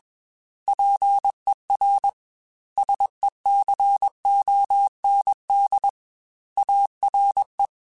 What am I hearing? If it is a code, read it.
Morse code, heard as a keyed tone, about 21 wpm, 800 Hz. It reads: PER SECOND ARE
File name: Technical Subjects / Technical Subjects-mp3s / XC @20